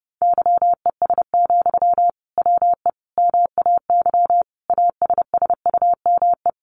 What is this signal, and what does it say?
Morse code, heard as a keyed tone, about 30 wpm, 700 Hz. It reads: YES, WE MAY ASSUME
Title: Sentences from Top 1500 Words 30wpm